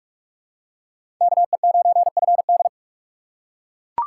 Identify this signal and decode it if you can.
Morse code, heard as a keyed tone, about 40 wpm, 700 Hz. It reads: KE9FD